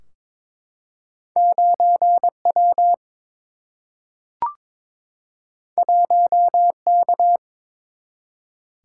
Morse code, heard as a keyed tone, about 23 wpm, 700 Hz. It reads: 9 W 1 K
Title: Single-Letter Single-Number 22wpm